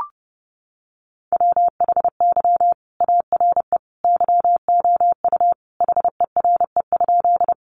Morse code, heard as a keyed tone, about 29 wpm, 700 Hz. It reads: WHY ARE YOU HERE?
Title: Sentences from Top 200 Words 30wpm